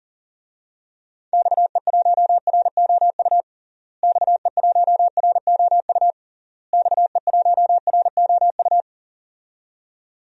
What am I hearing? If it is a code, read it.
Morse code, heard as a keyed tone, about 40 wpm, 700 Hz. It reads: XE1ROU XE1ROU XE1ROU
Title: DXCC-100 Call Signs - 6 Characters - Repeated 3x - 40wpm